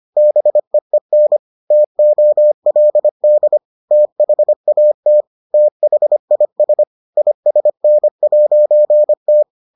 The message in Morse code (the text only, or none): BEEN TOLD THAT THIS ISN'T